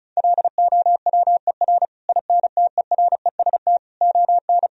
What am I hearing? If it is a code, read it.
Morse code, heard as a keyed tone, about 36 wpm, 700 Hz. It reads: LOWER INTEREST ON
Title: Sentences from Top 1000 Words - Review Part 4 - 35wpm